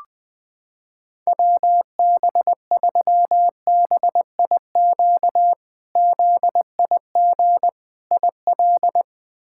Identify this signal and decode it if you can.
Morse code, heard as a keyed tone, about 20 wpm, 700 Hz. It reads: WB3BIQ ZIG IL
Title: K1USN SST QSO Practice - 20wpm